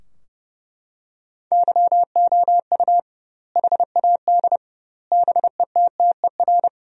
Morse code, heard as a keyed tone, about 30 wpm, 700 Hz. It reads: YOU HAD BETTER
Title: 2x Word-Spacing - Sentences from Top 400 Words - 30wpm